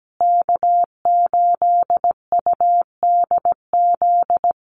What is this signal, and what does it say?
Morse code, heard as a keyed tone, about 17 wpm, 700 Hz. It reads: K8UDZ
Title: USA Call Signs Repeated 3x 17wpm